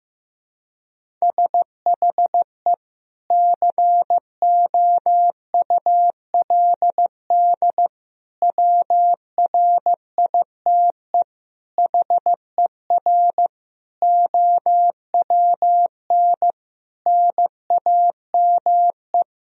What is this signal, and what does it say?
Morse code, heard as a keyed tone, about 15 wpm, 700 Hz. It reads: SHE COULD WRITE HER OWN NAME
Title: Sentences from Top 1000 Words - Review Part 1 - 15wpm